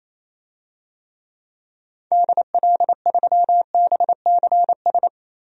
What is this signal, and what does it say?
Morse code, heard as a keyed tone, about 28 wpm, 700 Hz. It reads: DL3BCS
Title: DXCC-100 Call Signs - 6 Characters - 28wpm